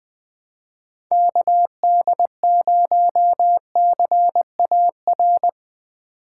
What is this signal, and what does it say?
Morse code, heard as a keyed tone, about 20 wpm, 700 Hz. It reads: KD0CAR
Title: USA Call Signs - 6x Character Spacing - 20wpm